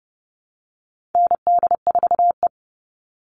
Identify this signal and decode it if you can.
Morse code, heard as a keyed tone, about 30 wpm, 700 Hz. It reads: ND4E
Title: All Cut Numbers - 4-Digit 30wpm